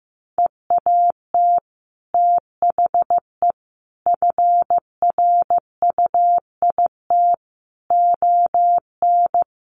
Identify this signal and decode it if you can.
Morse code, heard as a keyed tone, about 15 wpm, 700 Hz. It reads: EAT THE FRUIT ON
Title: Sentences from Top 1300 Words 15wpm